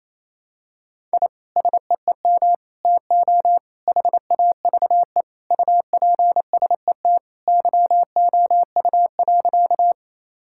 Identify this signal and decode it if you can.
Morse code, heard as a keyed tone, about 27 wpm, 700 Hz. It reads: I SEEM TO HAVE UPSET YOU.
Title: Sentences from Top 1000 Words - Review Part 5 - 28wpm